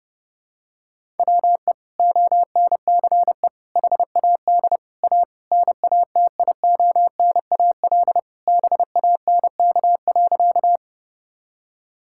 Morse code, heard as a keyed tone, about 30 wpm, 700 Hz. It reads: WE ONCE HAD A NATIONAL BANK.
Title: Sentences from Top 1000 Words - Review Part 6 - 30wpm